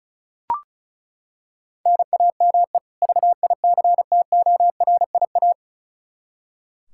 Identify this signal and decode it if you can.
Morse code, heard as a keyed tone, about 34 wpm, 700 Hz. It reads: NAME VICTORIA
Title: Practice QSO - Exchange 1 - Beginning - 35wpm